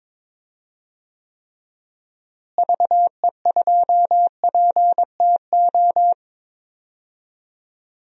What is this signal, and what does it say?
Morse code, heard as a keyed tone, about 22 wpm, 700 Hz. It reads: VE2PTO
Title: DXCC-100 Call Signs - 6 Characters - Repeated 3x - 22wpm